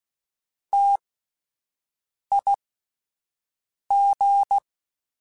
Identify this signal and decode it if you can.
Morse code, heard as a keyed tone, about 16 wpm, spaced as for 5 wpm, 800 Hz. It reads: TIG